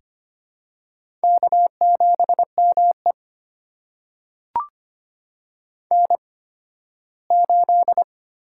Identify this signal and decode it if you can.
Morse code, heard as a keyed tone, about 25 wpm, spaced as for 15 wpm, 700 Hz. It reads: K7ME N8
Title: USA Call Signs - 4 Characters - 8x Character Spacing - 25wpm